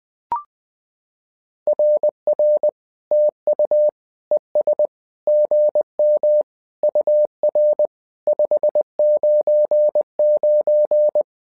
Morse code, heard as a keyed tone, about 20 wpm, 600 Hz. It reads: RR TU ES GM UR 599
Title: SOTA QSO Practice 20wpm